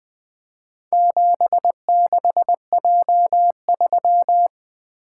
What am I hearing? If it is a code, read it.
Morse code, heard as a keyed tone, about 20 wpm, 700 Hz. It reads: 76J3
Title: Four Letter-Number Combinations - 20wpm